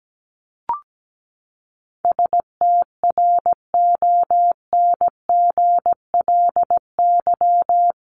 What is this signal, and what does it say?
Morse code, heard as a keyed tone, about 17 wpm, 700 Hz. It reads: STRONGLY
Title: LY Suffix - Top 100 Words with Base Words 17wpm